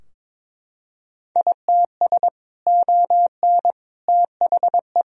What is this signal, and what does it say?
Morse code, heard as a keyed tone, about 22 wpm, 700 Hz. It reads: ITS ON THE
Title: Sentences from Top 1000 Words - Review Part 2 - 22wpm